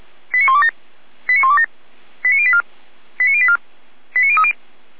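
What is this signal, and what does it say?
Five-tone selective call (ZVEI), heard as quick runs of five tones, tones of about 70 ms. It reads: 89127 89127 89084 89084 89030
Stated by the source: Five-tone ZVEI selective call received on SDR.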